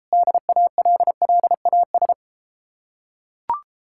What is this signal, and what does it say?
Morse code, heard as a keyed tone, about 33 wpm, 700 Hz. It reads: DALLAS